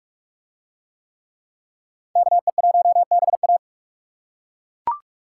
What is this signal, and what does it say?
Morse code, heard as a keyed tone, about 39 wpm, 700 Hz. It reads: KE1BA